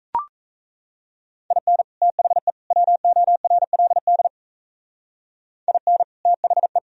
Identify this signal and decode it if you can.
Morse code, heard as a keyed tone, about 34 wpm, 700 Hz. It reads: IN THE WORLD IN THE
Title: Speed-Racing - Sets of 3 Words 28wpm